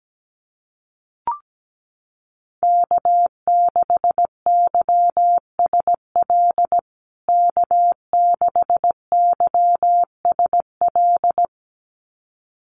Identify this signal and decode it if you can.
Morse code, heard as a keyed tone, about 17 wpm, 700 Hz. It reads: K6YSL K6YSL
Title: USA Call Signs Repeated 2x 17wpm